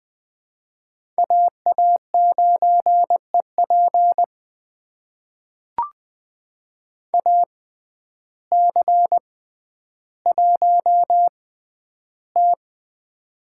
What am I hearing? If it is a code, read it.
Morse code, heard as a keyed tone, about 20 wpm, spaced as for 11 wpm, 700 Hz. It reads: AA9EP AC1T
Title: USA Call Signs - 6x Character Spacing - 20wpm